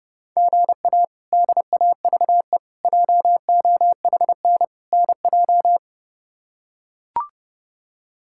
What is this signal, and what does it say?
Morse code, heard as a keyed tone, about 30 wpm, 700 Hz. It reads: GA DAVE JOHN NJ
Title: K1USN SST QSO Practice - 30wpm